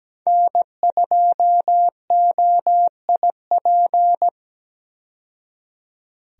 Morse code, heard as a keyed tone, about 17 wpm, 700 Hz. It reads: N2OIP
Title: USA Call Signs - 5 Characters - 17wpm